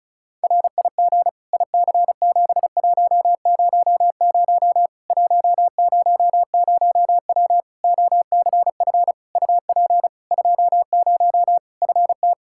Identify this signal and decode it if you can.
Morse code, heard as a keyed tone, about 34 wpm, 700 Hz. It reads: RIG IC7100 100W OCF UP 20 FT
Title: Practice QSO - Exchange 2 - Info - 35wpm